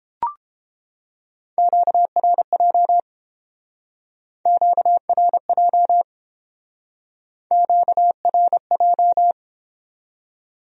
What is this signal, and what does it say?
Morse code, heard as a keyed tone, about 30 wpm, 700 Hz. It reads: QRJ QRJ QRJ